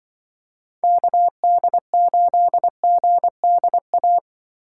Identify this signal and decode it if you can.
Morse code, heard as a keyed tone, about 24 wpm, 700 Hz. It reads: KD8GDA
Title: Speed Racing - USA Call Signs 20wpm